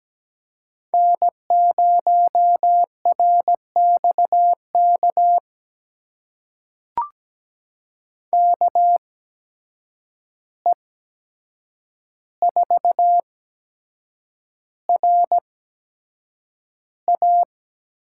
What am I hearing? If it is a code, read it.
Morse code, heard as a keyed tone, about 17 wpm, spaced as for 7 wpm, 700 Hz. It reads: N0RXK KE4RA
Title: USA Call Signs - 5 Characters - 8x Character Spacing - 17wpm